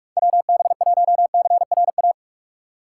Morse code, heard as a keyed tone, about 46 wpm, 700 Hz. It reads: WB1CRA
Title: USA Call Signs - 6 Characters - 2x Character Spacing - 45wpm